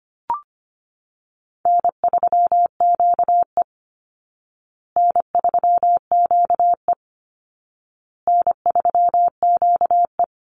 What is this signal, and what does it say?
Morse code, heard as a keyed tone, about 25 wpm, 700 Hz. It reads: N3QE N3QE N3QE
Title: Easy Call Signs Repeated 3x - 25wpm